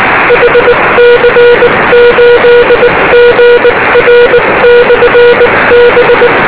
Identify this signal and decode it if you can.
Morse code, heard as a keyed tone, about 19 wpm, 480 Hz, with noise. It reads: HC8GR/B
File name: beacons / hc8gr